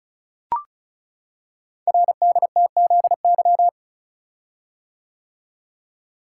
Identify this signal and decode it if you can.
Morse code, heard as a keyed tone, about 36 wpm, 700 Hz. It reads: RDTZY
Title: Code Groups - 5 Letters - 35wpm